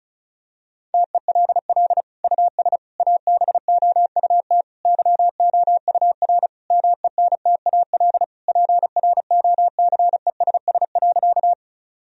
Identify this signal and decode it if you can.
Morse code, heard as a keyed tone, about 36 wpm, 700 Hz. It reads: TELL US ABOUT YOUR MENTAL PROCESS.
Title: Sentences from Top 1000 Words - Review Part 4 - 35wpm